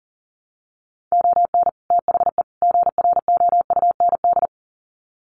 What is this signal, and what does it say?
Morse code, heard as a keyed tone, about 40 wpm, 700 Hz. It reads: ON THE GROUND
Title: Sets of 3 Words 40wpm